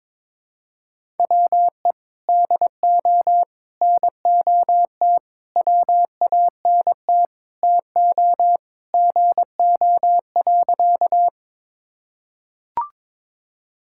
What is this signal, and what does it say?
Morse code, heard as a keyed tone, about 22 wpm, 700 Hz. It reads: WE DO NOT WANT TO GO.
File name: practice / 22wpm / Sentences with 4-Word Phrases - 22wpm